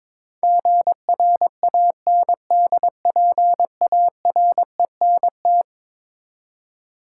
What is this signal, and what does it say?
Morse code, heard as a keyed tone, about 23 wpm, 700 Hz. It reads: GRANDPARENT
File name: practice / 22wpm / Eleven Letter Words - 22wpm